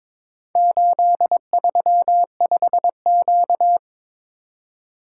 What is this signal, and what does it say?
Morse code, heard as a keyed tone, about 22 wpm, 700 Hz. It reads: 835Q
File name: practice / 22wpm / Four Letter-Number Combinations - 22wpm